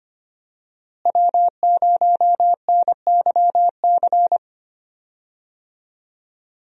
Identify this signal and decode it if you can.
Morse code, heard as a keyed tone, about 25 wpm, 700 Hz. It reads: W0NYC
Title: USA Call Signs - 5 Characters - Repeated 2x - 25wpm